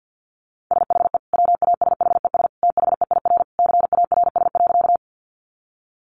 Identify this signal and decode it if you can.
Morse code, heard as a keyed tone, about 48 wpm, 700 Hz. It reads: SHE WASHES THEIR CARS.
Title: Easy Phrases 50wpm